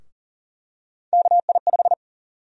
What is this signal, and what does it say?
Morse code, heard as a keyed tone, about 40 wpm, 700 Hz. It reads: KI5
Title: USA Call Sign Prefixes - 40wpm